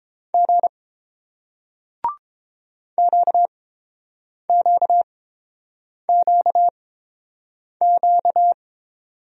Morse code, heard as a keyed tone, about 30 wpm, spaced as for 7 wpm, 700 Hz. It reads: G QQQQ